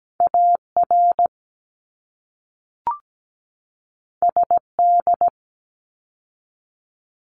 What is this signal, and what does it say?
Morse code, heard as a keyed tone, about 17 wpm, 700 Hz. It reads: AR SD
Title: ARRL Sections 17wpm